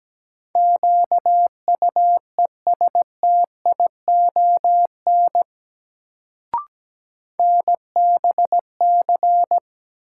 Morse code, heard as a keyed tone, about 17 wpm, 700 Hz. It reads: QUESTION NBC
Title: Sets of 5 Words Encore - New Word Drill - 17wpm